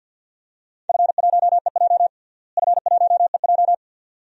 Morse code, heard as a keyed tone, about 51 wpm, 700 Hz. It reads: F1EJ F1EJ
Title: DXCC-100 Call Signs - 4 Characters - Repeated 2x - 50wpm